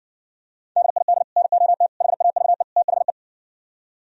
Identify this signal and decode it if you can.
Morse code, heard as a keyed tone, about 68 wpm, 700 Hz. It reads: DID NOT HAVE THE